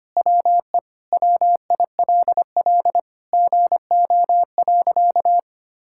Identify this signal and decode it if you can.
Morse code, heard as a keyed tone, about 25 wpm, 700 Hz. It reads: WE WILL GO.
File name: practice / 25wpm / Sentences with 4-Word Phrases - 25wpm